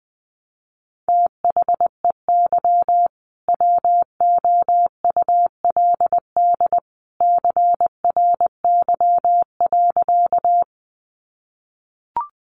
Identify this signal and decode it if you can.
Morse code, heard as a keyed tone, about 20 wpm, 700 Hz. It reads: THEY WOULD CRY.
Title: Sentences from Top 500 Words 20wpm